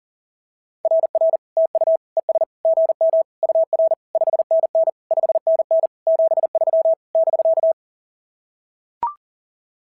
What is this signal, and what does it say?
Morse code, heard as a keyed tone, about 42 wpm, 690 Hz. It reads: RR TU ES GM UR 5NN 5NN 73 <BK>
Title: SOTA QSO Practice 40wpm